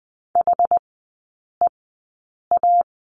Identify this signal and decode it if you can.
Morse code, heard as a keyed tone, about 19 wpm, spaced as for 8 wpm, 700 Hz. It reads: HEA
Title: 087 Lesson 15e - Number 5 - Three Characters 20wpm